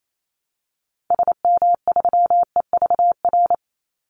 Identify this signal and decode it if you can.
Morse code, heard as a keyed tone, about 28 wpm, 700 Hz. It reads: SM3EVR
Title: Call Signs Repeated 2x - 28wpm